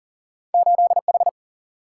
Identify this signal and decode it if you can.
Morse code, heard as a keyed tone, about 39 wpm, 700 Hz. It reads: 8H
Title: Single-Letter Single-Number 40wpm